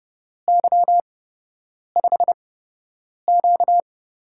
Morse code, heard as a keyed tone, about 30 wpm, spaced as for 8 wpm, 700 Hz. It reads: Y5Q
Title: USA Call Signs - 8x Character Spacing - 30wpm